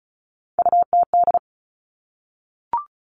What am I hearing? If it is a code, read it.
Morse code, heard as a keyed tone, about 36 wpm, 700 Hz. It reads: UTD
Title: Three Challenging Letter-Number Combinations 35wpm